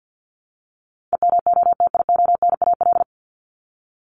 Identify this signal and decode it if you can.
Morse code, heard as a keyed tone, about 50 wpm, 700 Hz. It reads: EMOTIONAL
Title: Top 501-750 Words 50wpm